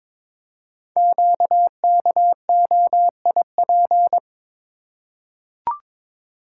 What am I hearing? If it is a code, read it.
Morse code, heard as a keyed tone, about 22 wpm, 700 Hz. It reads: QKOIP
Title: Code Groups - 5 Letters - 22wpm